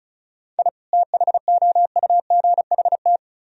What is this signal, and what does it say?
Morse code, heard as a keyed tone, about 43 wpm, 700 Hz. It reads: I THOUGHT
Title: Speed-Racing - Sets of 2 Words 35wpm